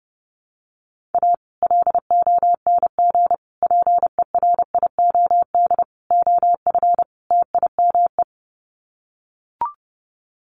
Morse code, heard as a keyed tone, about 30 wpm, 700 Hz. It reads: A LONG PERIOD OF TIME
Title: Sets of 5 Words 30wpm